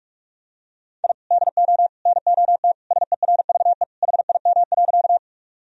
Morse code, heard as a keyed tone, about 46 wpm, 700 Hz. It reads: I DO NOT SERVE HIM.
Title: Sentences from Top 1000 Words - Review Part 2 - 45wpm